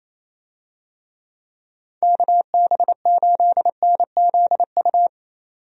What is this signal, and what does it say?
Morse code, heard as a keyed tone, about 28 wpm, 700 Hz. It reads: KB8NZU
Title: USA Call Signs - 6 Characters - 28wpm